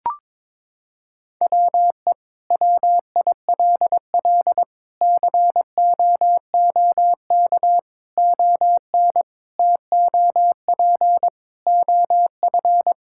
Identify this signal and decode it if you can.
Morse code, heard as a keyed tone, about 22 wpm, 700 Hz. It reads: WE WILL COOK ON TOP OF
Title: Sentences from Top 1000 Words - Review Part 6 - 22wpm